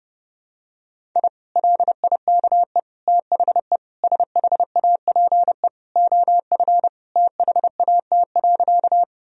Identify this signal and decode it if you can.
Morse code, heard as a keyed tone, about 30 wpm, 700 Hz. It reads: I LIKE THE SHAPE OF THAT.
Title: Sentences from Top 1000 Words - Review Part 3 - 30wpm